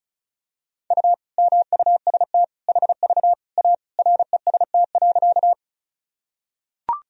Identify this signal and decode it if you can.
Morse code, heard as a keyed tone, about 36 wpm, 700 Hz. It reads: U MUST HV A REST.
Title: Sentences from Top 500 Words and Abbreviations 35wpm